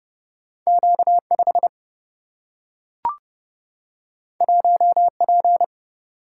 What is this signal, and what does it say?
Morse code, heard as a keyed tone, about 30 wpm, 700 Hz. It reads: Q 5 1 P
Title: Single-Letter Single-Number 30wpm